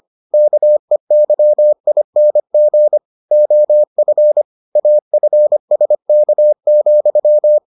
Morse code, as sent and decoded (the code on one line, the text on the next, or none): -.- . -.-- .. -. --. / --- ..-. / .- ..-. ... -.- --..--
KEYING OF AFSK,